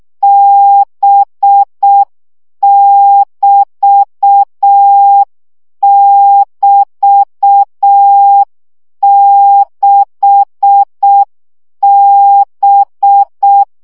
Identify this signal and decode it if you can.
Morse code, heard as a keyed tone, about 6 wpm, 800 Hz. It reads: B<BT><BT>6B
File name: CW gyakorlas b 6 egyenlo minusz 5 baud